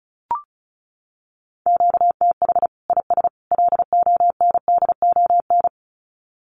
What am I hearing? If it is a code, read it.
Morse code, heard as a keyed tone, about 34 wpm, 700 Hz. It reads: QTH IS LONDON
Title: Common QSO Phrases 35wpm